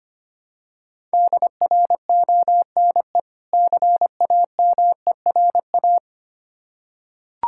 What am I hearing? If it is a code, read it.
Morse code, heard as a keyed tone, about 25 wpm, 700 Hz. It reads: DRONE CAMERA
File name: practice / 25wpm / Difficult Words with Hints - 25wpm